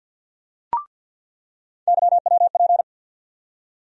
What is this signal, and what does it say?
Morse code, heard as a keyed tone, about 47 wpm, 700 Hz. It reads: YWP
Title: USA Call Sign Postfixes - 3 Letters 50wpm